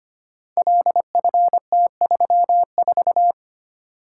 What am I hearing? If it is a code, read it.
Morse code, heard as a keyed tone, about 25 wpm, 700 Hz. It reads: LFT34
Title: Code Groups - 5 Letter:Numbers - 25wpm